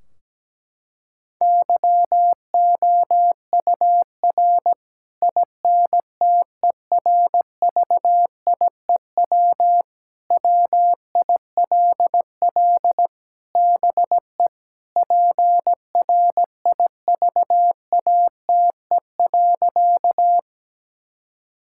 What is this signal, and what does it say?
Morse code, heard as a keyed tone, about 17 wpm, 700 Hz. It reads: YOUR INTERVIEW WILL BE PRIVATE.
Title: Sentences from Top 1600 Words 17wpm